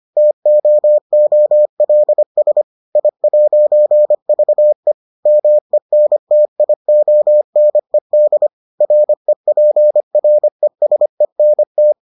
Morse code, heard as a keyed tone, about 25 wpm, 600 Hz. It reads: TOOLS I'VE MENTIONED REPRESENT